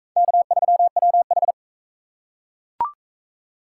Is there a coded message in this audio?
Yes: Morse code from 0.2 to 1.5 s, heard as a keyed tone, about 38 wpm, 700 Hz. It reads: K3WH